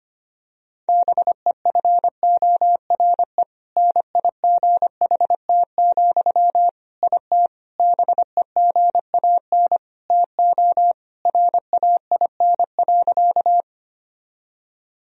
Morse code, heard as a keyed tone, about 25 wpm, 700 Hz. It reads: BEFORE NIGHT, IT BEGAN TO RAIN.
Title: Sentences from Top 1000 Words - Review Part 1 - 25wpm